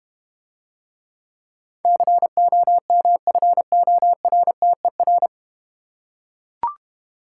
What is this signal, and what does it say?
Morse code, heard as a keyed tone, about 30 wpm, 700 Hz. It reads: COMFORTER
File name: Easy Words - Speed-Racing with Hints - 25wpm